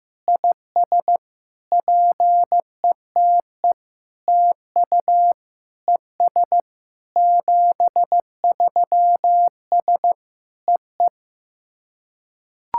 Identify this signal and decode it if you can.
Morse code, heard as a keyed tone, about 15 wpm, 700 Hz. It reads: IS PETE TU ES 73S EE
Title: SOTA QSO Practice 15wpm